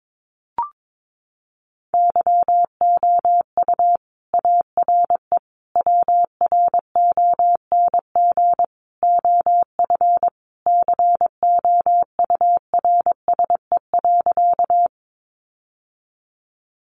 Morse code, heard as a keyed tone, about 22 wpm, 700 Hz. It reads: YOU ARE WRONG OF COURSE.
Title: Sentences from Top 900 Words 22wpm